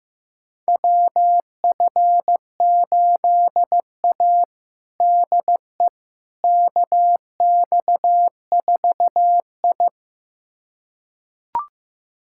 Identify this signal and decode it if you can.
Morse code, heard as a keyed tone, about 15 wpm, 700 Hz. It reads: WF8A DE KX4I
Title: USA Callsign DE Callsign - 4 Characters - 15wpm